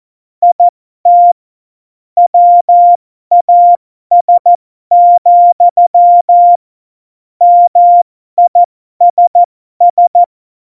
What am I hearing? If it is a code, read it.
Morse code, heard as a keyed tone, about 14 wpm, spaced as for 12 wpm, 700 Hz. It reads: IT WAS, MISS
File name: Morsekurs / Witze